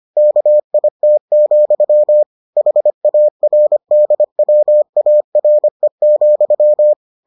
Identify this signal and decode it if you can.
Morse code, heard as a keyed tone, about 25 wpm, 600 Hz. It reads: KIT, HARDWARE,